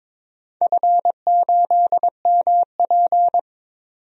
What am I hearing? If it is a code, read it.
Morse code, heard as a keyed tone, about 22 wpm, 700 Hz. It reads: F8MP
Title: DXCC-100 Call Signs - 4 Characters - 22wpm